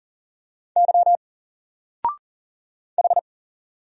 Morse code, heard as a keyed tone, about 40 wpm, 700 Hz. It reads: Y H
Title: Single Character (Letter:Number:Punctuation:Prosign) Alt 40wpm